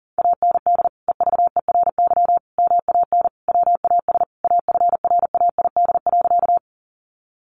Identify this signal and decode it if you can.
Morse code, heard as a keyed tone, about 40 wpm, 700 Hz. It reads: AND EVERY MAN WAS AFRAID.
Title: Sentences from Top 1500 Words 40wpm